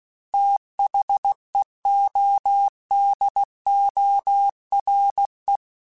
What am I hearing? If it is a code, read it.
Morse code, heard as a keyed tone, about 16 wpm, 800 Hz. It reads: THEODORE
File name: TH Runs / TH Runs mp3s / TH Runs @15